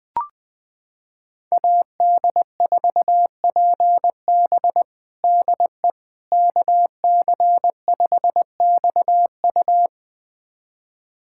Morse code, heard as a keyed tone, about 21 wpm, 700 Hz. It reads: AD4PB DE KC5XU
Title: USA Callsign DE Callsign - 5 Characters - 20wpm